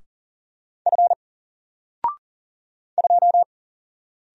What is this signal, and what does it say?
Morse code, heard as a keyed tone, about 40 wpm, 700 Hz. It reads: F 2
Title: Single Character (Letter:Number:Punctuation:Prosign) 40wpm